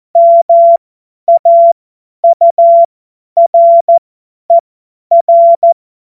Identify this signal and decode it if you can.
Morse code, heard as a keyed tone, about 14 wpm, spaced as for 10 wpm, 680 Hz. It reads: MAURER
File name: Morsekurs / Witze / Witz 16. Dezember 2025 50 F70